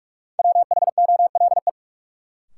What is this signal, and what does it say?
Morse code, heard as a keyed tone, about 43 wpm, 700 Hz. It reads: WHOLE
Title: Sets of 2 Words - New Word Drill - 45wpm